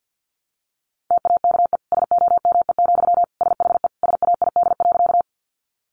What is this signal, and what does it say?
Morse code, heard as a keyed tone, about 50 wpm, 700 Hz. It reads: TAKE SOME, SHE SAID.
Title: Sentences from Top 200 Words 50wpm